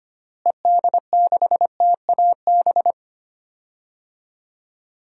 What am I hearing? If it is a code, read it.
Morse code, heard as a keyed tone, about 25 wpm, 700 Hz. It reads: ED6TAB
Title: DXCC-100 Call Signs - 6 Characters - Repeated 3x - 25wpm